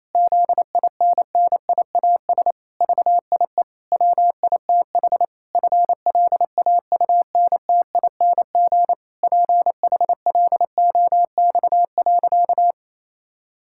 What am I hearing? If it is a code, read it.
Morse code, heard as a keyed tone, about 27 wpm, 700 Hz. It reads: ZINNIAS VIE WITH FLAUNTING PHLOX.
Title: Pangram Challenge v2 - 28wpm